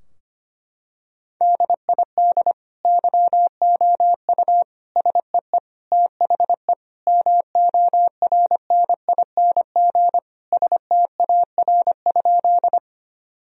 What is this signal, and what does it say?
Morse code, heard as a keyed tone, about 25 wpm, 700 Hz. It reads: DID YOU SEE THE MORNING STAR?
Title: Sentences from Top 1000 Words - Review Part 2 - 25wpm